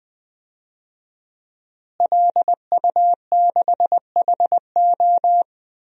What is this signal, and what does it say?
Morse code, heard as a keyed tone, about 21 wpm, 700 Hz. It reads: LU6HO
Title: DXCC-100 Call Signs - 5 Characters - Repeated 3x - 20wpm